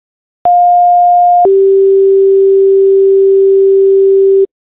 Two-tone page, heard as a one-second tone, then a three-second tone, about 710 then 390 Hz. It is heard from 0.4 to 4.5 s.